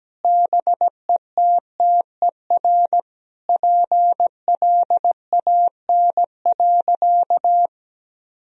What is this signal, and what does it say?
Morse code, heard as a keyed tone, about 17 wpm, 700 Hz. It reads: BETTER PLAN.
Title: Sentences from Top 1000 Words - Review Part 2 - 17wpm